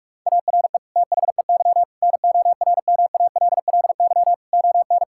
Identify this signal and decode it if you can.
Morse code, heard as a keyed tone, about 46 wpm, 700 Hz. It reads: ARE THEY NORMALLY ON
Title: Sentences from Top 1000 Words - Review Part 6 - 45wpm